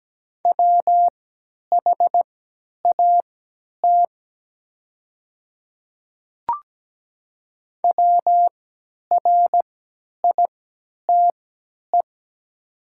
Morse code, heard as a keyed tone, about 17 wpm, spaced as for 10 wpm, 700 Hz. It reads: WHAT WRITE
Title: 3x Character Spacing - Top 100 Words - 17wpm